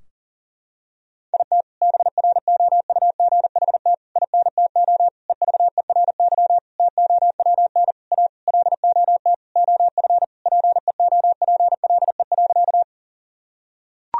Morse code, heard as a keyed tone, about 40 wpm, 700 Hz. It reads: IT BROUGHT INTO EVERY TOWN A LOT OF PEOPLE.
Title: Sentences from Top 1000 Words - Review Part 3 - 40wpm